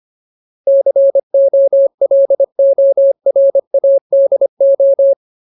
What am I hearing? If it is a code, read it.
Morse code, heard as a keyed tone, about 25 wpm, 550 Hz. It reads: COLORADO
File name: Ultimate Word List / 25WPM Ultimate